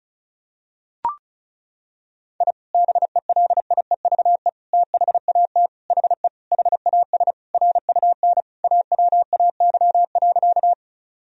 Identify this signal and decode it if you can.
Morse code, heard as a keyed tone, about 36 wpm, 700 Hz. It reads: I BELIEVE THAT HE HAS RUN AWAY.
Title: Sentences from Top 1000 Words - Review Part 1 - 35wpm